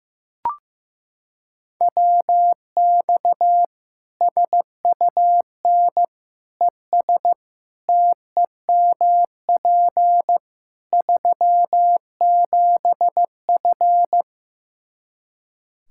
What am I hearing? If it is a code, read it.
Morse code, heard as a keyed tone, about 15 wpm, 700 Hz. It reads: WX SUN ES TEMP 37F
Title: K1USN SST QSO Practice 15wpm